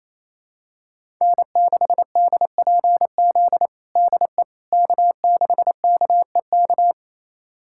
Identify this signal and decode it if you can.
Morse code, heard as a keyed tone, about 28 wpm, 700 Hz. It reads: N6DPZ DE K6KEK
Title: USA Callsign DE Callsign - 5 Characters - 28wpm